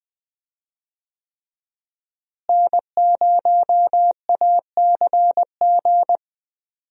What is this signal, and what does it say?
Morse code, heard as a keyed tone, about 20 wpm, 700 Hz. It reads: N0ACG
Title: USA Call Signs - 5 Characters - Repeated 3x - 20wpm